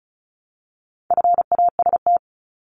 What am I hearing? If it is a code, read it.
Morse code, heard as a keyed tone, about 34 wpm, 700 Hz. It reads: FAST
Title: Top 500 Words 35wpm